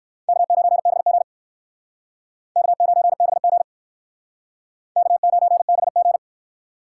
Morse code, heard as a keyed tone, about 57 wpm, 700 Hz. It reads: K9BG K9BG K9BG